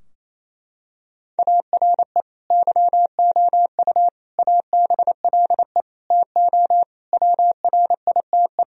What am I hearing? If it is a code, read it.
Morse code, heard as a keyed tone, about 28 wpm, 700 Hz. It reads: ARE YOU ABLE TO WRITE
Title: Sentences from Top 1000 Words - Review Part 3 - 28wpm